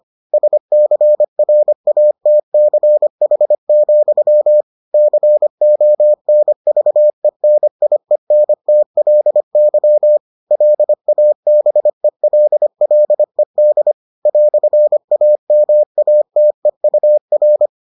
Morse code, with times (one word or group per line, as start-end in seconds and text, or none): 0.3-4.6 SCRATCH,
4.9-10.2 CONVENIENTLY
10.5-13.9 LABELLED
14.3-17.7 "AMATEUR